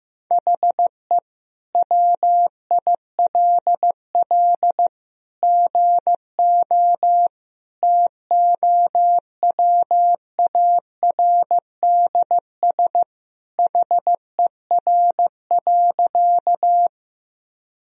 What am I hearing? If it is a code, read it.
Morse code, heard as a keyed tone, about 14 wpm, 700 Hz. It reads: HE WILL GO TOWARDS HER.
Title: Sentences from Top 1000 Words - Review Part 2 - 15wpm